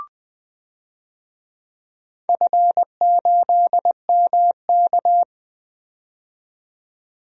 Morse code, heard as a keyed tone, about 20 wpm, 700 Hz. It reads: F8MK
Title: DXCC-100 Call Signs - 4 Characters - 20wpm